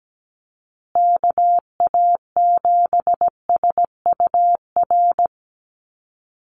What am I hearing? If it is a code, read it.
Morse code, heard as a keyed tone, about 17 wpm, 700 Hz. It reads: KA7SUR
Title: USA Call Signs Repeated 2x 17wpm